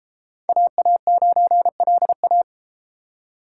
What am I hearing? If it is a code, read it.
Morse code, heard as a keyed tone, about 33 wpm, 700 Hz. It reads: AA9LA